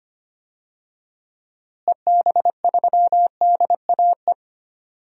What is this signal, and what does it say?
Morse code, heard as a keyed tone, about 25 wpm, 700 Hz. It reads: EB3DAE
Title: DXCC-100 Call Signs - 6 Characters - Repeated 3x - 25wpm